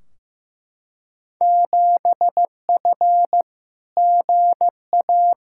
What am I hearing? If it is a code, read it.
Morse code, heard as a keyed tone, about 15 wpm, 700 Hz. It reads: 7F GA
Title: Field Day Exchanges Encore - 15wpm